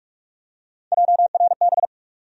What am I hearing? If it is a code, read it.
Morse code, heard as a keyed tone, about 45 wpm, 700 Hz. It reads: JRB
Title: USA Call Sign Postfixes - 3 Letters 45wpm